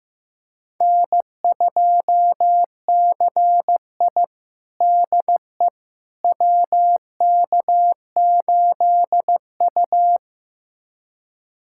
Morse code, heard as a keyed tone, about 15 wpm, 700 Hz. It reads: N2CI DE WK8U
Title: USA Callsign DE Callsign - 4 Characters - 15wpm